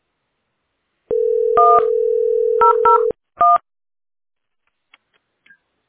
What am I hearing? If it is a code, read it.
DTMF, heard as telephone touch tones, tones of about 160 ms. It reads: 1001